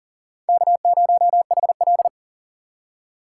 Morse code, heard as a keyed tone, about 40 wpm, 700 Hz. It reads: K0HL
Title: USA Call Signs - 4 Characters - 3x Character Spacing - 40wpm